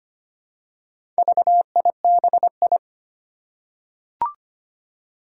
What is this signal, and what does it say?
Morse code, heard as a keyed tone, about 25 wpm, 700 Hz. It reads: VIBI
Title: Four Challenging Letter-Number Combinations 25wpm